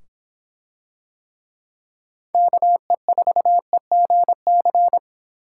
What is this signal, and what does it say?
Morse code, heard as a keyed tone, about 26 wpm, 700 Hz. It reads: KE4EGC